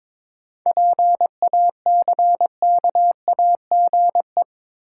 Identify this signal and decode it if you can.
Morse code, heard as a keyed tone, about 22 wpm, 700 Hz. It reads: PACKAGE
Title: Seven Letter Words - 22wpm